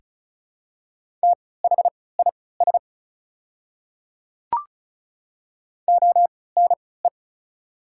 Morse code, heard as a keyed tone, about 34 wpm, spaced as for 20 wpm, 700 Hz. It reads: THIS ONE
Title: 3x Character Spacing - Top 100 Words - 35wpm